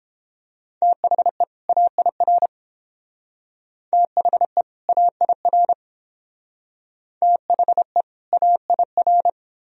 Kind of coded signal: Morse code

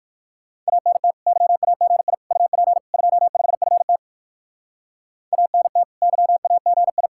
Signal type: Morse code

